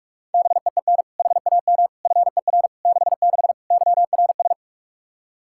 Morse code, heard as a keyed tone, about 46 wpm, 700 Hz. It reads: BEEN HAM FER 66 YRS
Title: Practice QSO - Exchange 3 - More Info - 45wpm